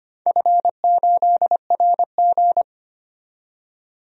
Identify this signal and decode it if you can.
Morse code, heard as a keyed tone, about 25 wpm, 700 Hz. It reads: F8RG
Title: DXCC-100 Call Signs - 4 Characters - 25wpm